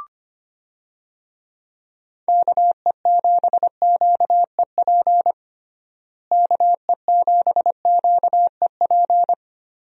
Morse code, heard as a keyed tone, about 25 wpm, 700 Hz. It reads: KE7QEP KE7QEP
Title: USA Call Signs - 6 Characters - Repeated 3x - 25wpm